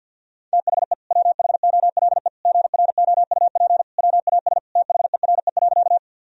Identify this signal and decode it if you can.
Morse code, heard as a keyed tone, about 51 wpm, 700 Hz. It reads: THE WHOLE GROUP WAS THERE.
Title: Sentences from Top 1000 Words - Review Part 1 - 50wpm